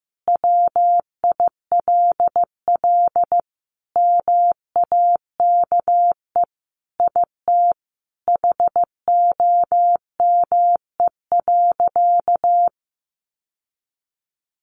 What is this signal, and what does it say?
Morse code, heard as a keyed tone, about 15 wpm, 700 Hz. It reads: WILL MAKE IT HOME.
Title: Sentences from Top 1500 Words 15wpm